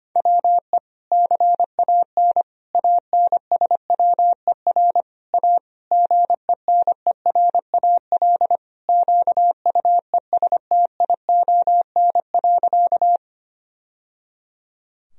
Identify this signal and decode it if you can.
Morse code, heard as a keyed tone, about 25 wpm, 700 Hz. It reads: WE CAN ANSWER A GENERAL QUESTION.
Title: Sentences from Top 1000 Words - Review Part 3 - 25wpm